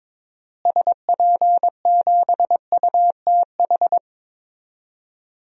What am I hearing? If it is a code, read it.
Morse code, heard as a keyed tone, about 22 wpm, 700 Hz. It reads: SP7UTH
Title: DXCC-100 Call Signs - 6 Characters - Repeated 2x - 22wpm